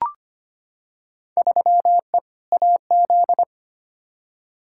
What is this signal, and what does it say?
Morse code, heard as a keyed tone, about 25 wpm, 700 Hz. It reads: 3E AZ
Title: Field Day Exchanges - 25wpm